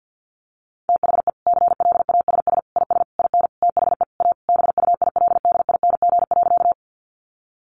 Morse code, heard as a keyed tone, about 50 wpm, 700 Hz. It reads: THE CLASS IS IN THE A BUILDING.
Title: Additional Easy Phrases 50wpm